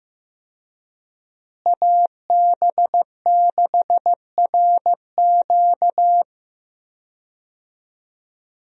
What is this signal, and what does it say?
Morse code, heard as a keyed tone, about 15 wpm, 700 Hz. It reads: AB6RQ
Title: USA Call Signs - 5 Characters - Repeated 3x - 15wpm